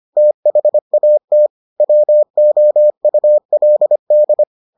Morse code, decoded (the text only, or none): THAT WOULD